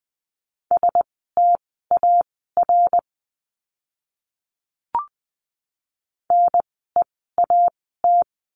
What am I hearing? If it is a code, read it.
Morse code, heard as a keyed tone, about 20 wpm, spaced as for 15 wpm, 700 Hz. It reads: STAR NEAT
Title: 054 Lesson 10f - Letter r - Words 20wpm